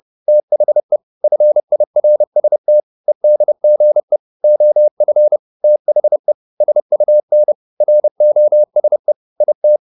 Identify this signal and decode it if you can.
Morse code, heard as a keyed tone, about 30 wpm, 600 Hz. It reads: THE FIRST EDGE OF THE SUN ROSE IT
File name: the old man and the sea / the old man and the sea 1 30wpm